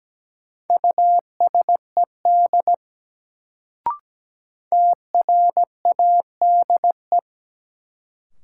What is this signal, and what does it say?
Morse code, heard as a keyed tone, about 17 wpm, 700 Hz. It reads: USED TRADE
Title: Sentences from Top 600 Words - New Word Drill - 17wpm